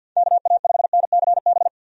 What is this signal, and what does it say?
Morse code, heard as a keyed tone, about 48 wpm, 700 Hz. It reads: KA5NCB
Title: USA Call Signs - 6 Characters - 50wpm